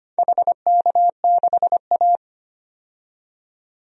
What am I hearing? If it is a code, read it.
Morse code, heard as a keyed tone, about 25 wpm, 700 Hz. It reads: HK6A
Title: DXCC-100 Call Signs - 4 Characters - 25wpm